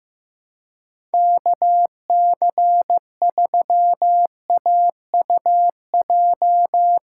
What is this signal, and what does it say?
Morse code, heard as a keyed tone, about 15 wpm, 700 Hz. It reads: KC3AUJ
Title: USA Callsign DE Callsign - 6 Characters - 15wpm